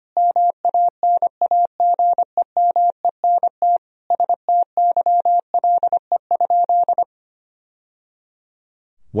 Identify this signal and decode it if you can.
Morse code, heard as a keyed tone, about 25 wpm, 700 Hz. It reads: MANAGEMENT STYLE?
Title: Sentences from Top 1000 Words - Review Part 4 - 25wpm